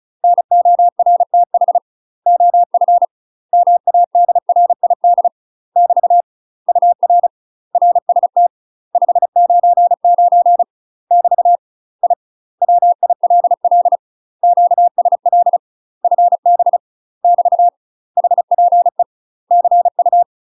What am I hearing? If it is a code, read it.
Morse code, heard as a keyed tone, about 36 wpm, 700 Hz. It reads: NORTH OF MADRID <BT> UR RST 599 <BT> I WILL QSL FB <BT> HPE CU
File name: UK MASTER QSOs TRAINING / 35WPM / UK MASTER QSOs 0011